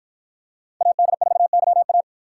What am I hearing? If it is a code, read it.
Morse code, heard as a keyed tone, about 53 wpm, 700 Hz. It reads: AD4YA